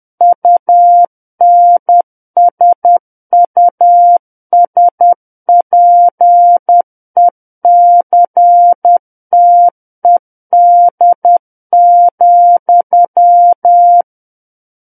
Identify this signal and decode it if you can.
Morse code, heard as a keyed tone, about 10 wpm, 700 Hz. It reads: UNSUSPECTED,
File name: Book One / 01-Chapter 1 - 10 WPM